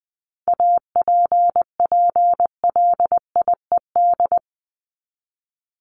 Morse code, heard as a keyed tone, about 20 wpm, 700 Hz. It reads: APPLIED
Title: ED Suffix - Top 100 Words with Base Words 20wpm